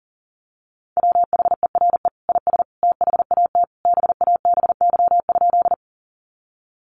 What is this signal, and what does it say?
Morse code, heard as a keyed tone, about 40 wpm, 700 Hz. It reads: WHERE IS THAT BABY?
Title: Sentences from Top 1300 Words 40wpm